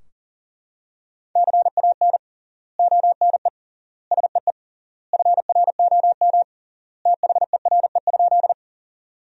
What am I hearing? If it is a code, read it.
Morse code, heard as a keyed tone, about 40 wpm, 700 Hz. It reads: CAN ONE SEE FROM THERE?
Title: 3x Word-Spacing - Sentences from Top 100 Words - 40wpm